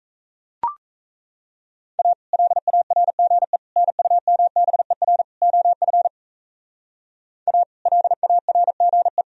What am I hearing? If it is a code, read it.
Morse code, heard as a keyed tone, about 35 wpm, 700 Hz. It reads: A LARGE NUMBER OF A LARGE